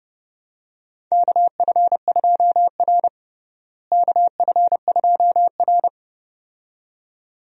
Morse code, heard as a keyed tone, about 30 wpm, 700 Hz. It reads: KF2R KF2R
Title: USA Call Signs - 4 Characters - Repeated 2x - 30wpm